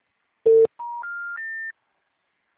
Special information tone.